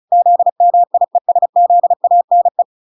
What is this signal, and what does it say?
Morse code, heard as a keyed tone, about 36 wpm, 700 Hz. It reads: ZMIESZANE